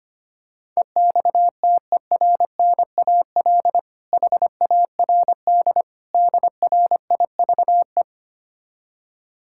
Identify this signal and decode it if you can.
Morse code, heard as a keyed tone, about 25 wpm, 700 Hz. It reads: EXTERNAL HARD DRIVE
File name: practice / 25wpm / Difficult Words with Hints - 25wpm